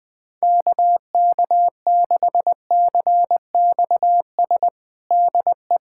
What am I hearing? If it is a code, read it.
Morse code, heard as a keyed tone, about 20 wpm, 700 Hz. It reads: KK6CXS DE
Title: USA Callsign DE Callsign - 6 Characters - 20wpm